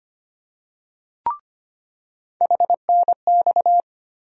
Morse code, heard as a keyed tone, about 25 wpm, 700 Hz. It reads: HNX
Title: USA Call Sign Postfixes - 3 Letters 25wpm